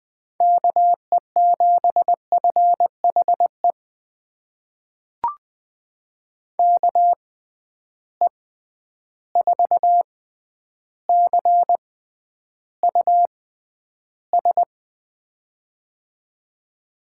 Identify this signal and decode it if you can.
Morse code, heard as a keyed tone, about 20 wpm, spaced as for 10 wpm, 700 Hz. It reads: KE7FHE KE4CUS
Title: USA Call Signs - 6 Characters - 6x Character Spacing - 20wpm